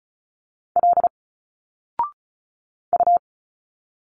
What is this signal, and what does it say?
Morse code, heard as a keyed tone, about 34 wpm, 700 Hz. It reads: L U
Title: Single Letters 35wpm